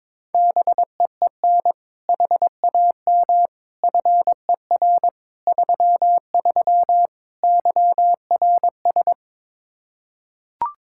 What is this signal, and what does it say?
Morse code, heard as a keyed tone, about 21 wpm, 700 Hz. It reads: BEEN HAM FER 33 YRS
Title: Practice QSO - Exchange 3 - More Info - 22wpm